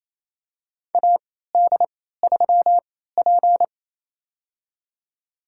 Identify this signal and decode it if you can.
Morse code, heard as a keyed tone, about 28 wpm, spaced as for 16 wpm, 700 Hz. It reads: AD3P
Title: USA Call Signs - 4 Characters - 3x Character Spacing - 28wpm